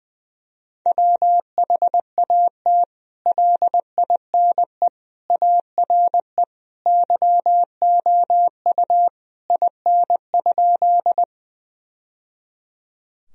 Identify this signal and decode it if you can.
Morse code, heard as a keyed tone, about 20 wpm, 700 Hz. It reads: WHAT LINE ARE YOU IN?
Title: Sentences from Top 1000 Words - Review Part 1 - 20wpm